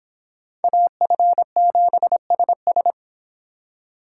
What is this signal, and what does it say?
Morse code, heard as a keyed tone, about 26 wpm, 700 Hz. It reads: AF7SS